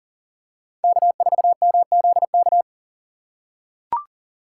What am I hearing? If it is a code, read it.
Morse code, heard as a keyed tone, about 39 wpm, 700 Hz. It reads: K4MZK
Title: USA Call Signs - 5 Characters - 6x Character Spacing - 40wpm